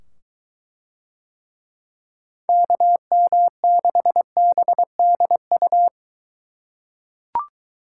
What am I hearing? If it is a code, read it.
Morse code, heard as a keyed tone, about 22 wpm, 700 Hz. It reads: KM6BDU